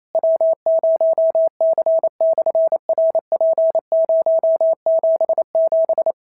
Morse code, heard as a keyed tone, about 28 wpm, 650 Hz. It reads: W0C/RP077
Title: SOTA QSO Practice 28wpm